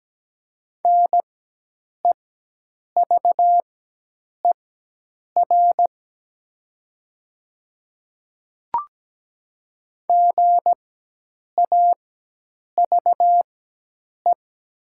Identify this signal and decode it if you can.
Morse code, heard as a keyed tone, about 17 wpm, spaced as for 8 wpm, 700 Hz. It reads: NEVER GAVE
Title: 4x Character Spacing - Top 500 Words - 17wpm